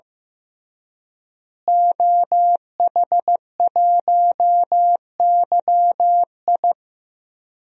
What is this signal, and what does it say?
Morse code, heard as a keyed tone, about 15 wpm, 700 Hz. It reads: OH1YI
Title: DXCC-100 Call Signs - 5 Characters - Repeated 2x - 15wpm